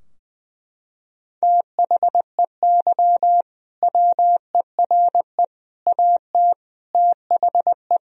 Morse code, heard as a keyed tone, about 20 wpm, 700 Hz. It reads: THEY WERE AT THE
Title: Sentences from Top 1000 Words - Review Part 3 - 20wpm